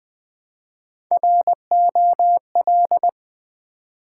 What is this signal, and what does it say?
Morse code, heard as a keyed tone, about 20 wpm, 700 Hz. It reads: ROL
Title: Build A Word - 1 Letter At A Time - 4-Letter Words - 20wpm